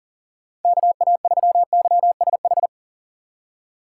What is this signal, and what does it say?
Morse code, heard as a keyed tone, about 39 wpm, 700 Hz. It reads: KA3YSH
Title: USA Call Signs - 6 Characters - 3x Character Spacing - 40wpm